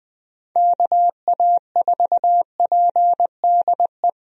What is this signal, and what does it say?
Morse code, heard as a keyed tone, about 20 wpm, 700 Hz. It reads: KA4PDE
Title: USA Call Signs - 6 Characters - Repeated 2x - 20wpm